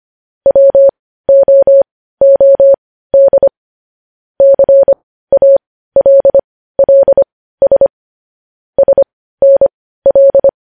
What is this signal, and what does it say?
Morse code, heard as a keyed tone, about 25 wpm, spaced as for 15 wpm, 550 Hz. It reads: WOOD CALLS SNL
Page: News Headlines in Morse Code at 15 WPM